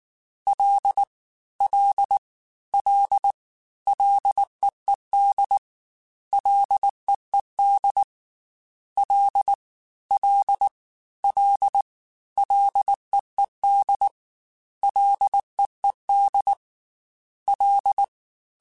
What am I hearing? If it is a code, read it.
Morse code, heard as a keyed tone, about 19 wpm, 800 Hz. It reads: L L L LEED LEED L L L LEED LEED L
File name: F and L / F and L mp3 / F and L Prefixes 3x2 Twice @18